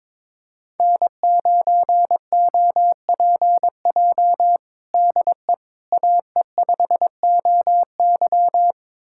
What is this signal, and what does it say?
Morse code, heard as a keyed tone, about 22 wpm, 700 Hz. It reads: N9OPJ DE AE5OY
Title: USA Callsign DE Callsign - 5 Characters - 22wpm